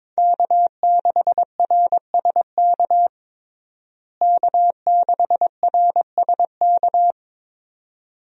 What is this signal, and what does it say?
Morse code, heard as a keyed tone, about 21 wpm, 700 Hz. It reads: K6RSK K6RSK
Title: USA Call Signs - 5 Characters - Repeated 2x - 22wpm